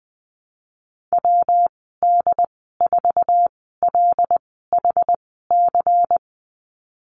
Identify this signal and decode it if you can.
Morse code, heard as a keyed tone, about 20 wpm, spaced as for 15 wpm, 700 Hz. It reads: WD4LHC
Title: 147 Lesson 24g - Letter y - Call Signs 20wpm